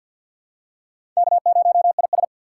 Morse code, heard as a keyed tone, about 47 wpm, 700 Hz. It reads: K0IS
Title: USA Call Signs - 3x Character Spacing - 50wpm